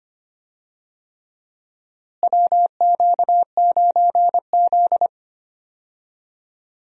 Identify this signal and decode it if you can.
Morse code, heard as a keyed tone, about 25 wpm, 700 Hz. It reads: WQ9Z
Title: USA Call Signs - 4 Characters - Repeated 3x - 25wpm